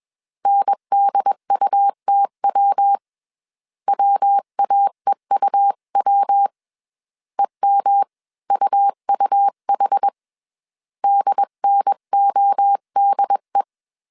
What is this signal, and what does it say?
Morse code, heard as a keyed tone, about 21 wpm, 800 Hz. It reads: NDUTW WAEUW EMUUH DNODE
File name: CW / DET / 1 3 1500C S B 20wpm 1754